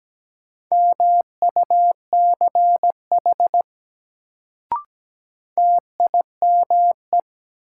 Morse code, heard as a keyed tone, about 17 wpm, 700 Hz. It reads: MUCH TIME
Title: Sets of 2 Words - New Word Drill - 17wpm